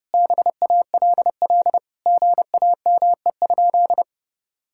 Morse code, heard as a keyed tone, about 30 wpm, 700 Hz. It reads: BALL GAME?
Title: Sentences from Top 1000 Words - Review Part 6 - 30wpm